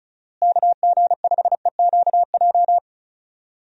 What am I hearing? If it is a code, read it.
Morse code, heard as a keyed tone, about 36 wpm, 700 Hz. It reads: KG5EQJ
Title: USA Call Signs - 6 Characters - Repeated 3x - 35wpm